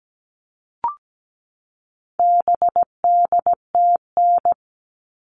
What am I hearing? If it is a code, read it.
Morse code, heard as a keyed tone, about 17 wpm, 700 Hz. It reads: BDTN
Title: All Cut Numbers - 4-Digit 17wpm